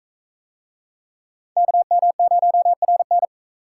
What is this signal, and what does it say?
Morse code, heard as a keyed tone, about 41 wpm, 700 Hz. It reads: KM0RN